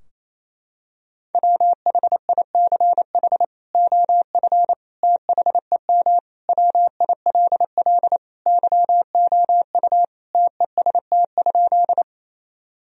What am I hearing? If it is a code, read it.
Morse code, heard as a keyed tone, about 28 wpm, 700 Hz. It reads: WHICH OF THEM WILL YOU TEST?
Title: Sentences from Top 1000 Words - Review Part 3 - 28wpm